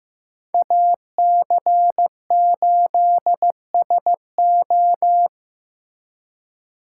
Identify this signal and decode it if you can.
Morse code, heard as a keyed tone, about 18 wpm, 700 Hz. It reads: AC8SO
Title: Speed Racing - USA 2x2 Call Signs 15wpm